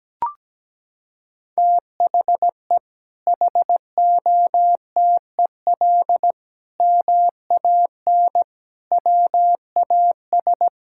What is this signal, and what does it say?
Morse code, heard as a keyed tone, about 17 wpm, 700 Hz. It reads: THE HOTEL MAN WAS
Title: Sentences from Top 1600 Words 17wpm